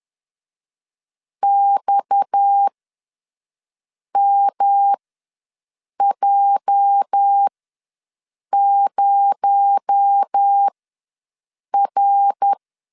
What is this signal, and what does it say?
Morse code, heard as a keyed tone, about 10 wpm, spaced as for 6 wpm, 800 Hz. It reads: XMJ0R
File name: LEZIONI / 8 YJ45 / 8 YJ4 ws 10wpm 4020